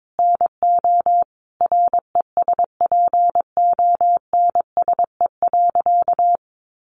Morse code, heard as a keyed tone, about 22 wpm, 700 Hz. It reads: NO RESPONSE.
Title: Sentences from Top 1400 Words 22wpm